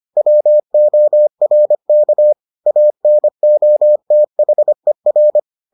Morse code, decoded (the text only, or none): WORK ANOTHER